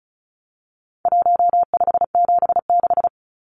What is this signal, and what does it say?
Morse code, heard as a keyed tone, about 34 wpm, 700 Hz. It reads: 1576
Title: Numbers - Contest Style - 4-Digit 35wpm